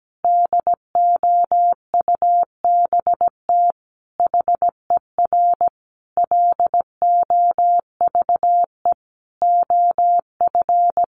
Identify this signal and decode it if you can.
Morse code, heard as a keyed tone, about 17 wpm, 700 Hz. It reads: DOUBT HER LOVE OF
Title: Sentences from Top 1500 Words 17wpm